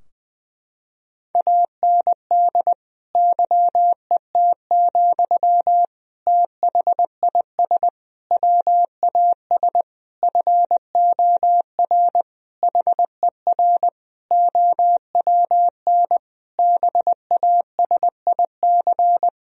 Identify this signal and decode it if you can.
Morse code, heard as a keyed tone, about 20 wpm, 700 Hz. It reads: AND YET, THIS WAS FOR HER OWN BASIC
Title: Sentences from Top 1000 Words - Review Part 3 - 20wpm